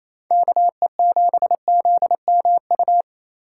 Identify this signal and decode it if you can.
Morse code, heard as a keyed tone, about 28 wpm, 700 Hz. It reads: KE7ZMU
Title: USA Call Signs - 2x Character Spacing - 28wpm